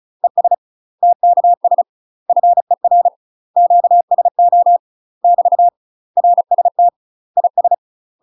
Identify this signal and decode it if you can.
Morse code, heard as a keyed tone, about 35 wpm, 700 Hz. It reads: ES TKS FER QSO <BT> RST IS
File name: UK MASTER QSOs TRAINING/35WPM